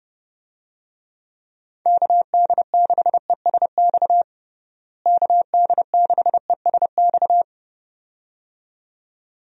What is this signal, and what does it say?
Morse code, heard as a keyed tone, about 30 wpm, 700 Hz. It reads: KD6ESX KD6ESX
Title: USA Call Signs - 6 Characters - Repeated 2x - 30wpm